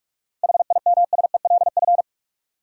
Morse code, heard as a keyed tone, about 46 wpm, 700 Hz. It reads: HIMSELF
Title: Sentences from Top 1500 Words - New Word Drill - 45wpm